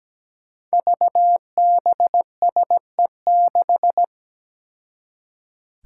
Morse code, heard as a keyed tone, about 17 wpm, 700 Hz. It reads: VBSE6
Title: Five Challenging Letter-Number Combinations 17wpm